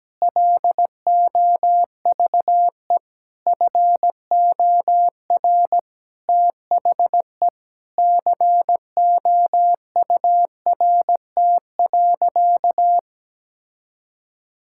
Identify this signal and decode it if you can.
Morse code, heard as a keyed tone, about 17 wpm, 700 Hz. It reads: LOVE FOR THE COURT.
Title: Sentences from Top 1600 Words 17wpm